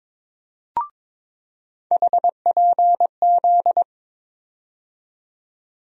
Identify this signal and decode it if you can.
Morse code, heard as a keyed tone, about 22 wpm, 700 Hz. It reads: HPZ
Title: USA Call Sign Postfixes - 3 Letters 22wpm